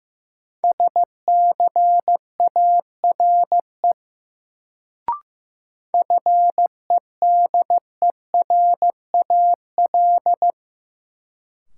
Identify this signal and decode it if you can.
Morse code, heard as a keyed tone, about 15 wpm, 700 Hz. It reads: SCARE FEDERAL
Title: Sentences from Top 1000 Words - Review Part 5 - New Word Drill - 15wpm